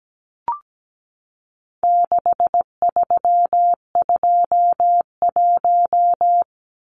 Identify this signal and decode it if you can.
Morse code, heard as a keyed tone, about 17 wpm, 700 Hz. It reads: 6321
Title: Numbers - 4-Digit 17wpm